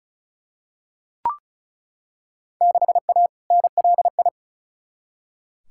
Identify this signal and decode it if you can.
Morse code, heard as a keyed tone, about 36 wpm, 690 Hz. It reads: 6A NLI
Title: Field Day Exchanges Encore - 35wpm